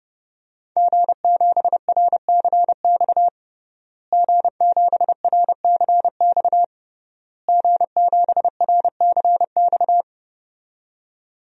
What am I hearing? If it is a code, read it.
Morse code, heard as a keyed tone, about 30 wpm, 700 Hz. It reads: G7RCX G7RCX G7RCX
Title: DXCC-100 Call Signs - 5 Characters - Repeated 3x - 30wpm